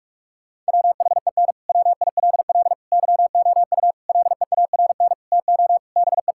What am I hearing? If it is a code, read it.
Morse code, heard as a keyed tone, about 41 wpm, 700 Hz. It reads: WHEN WILL YOU LEARN TO BE
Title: Sentences from Top 1000 Words - Review Part 6 - 45wpm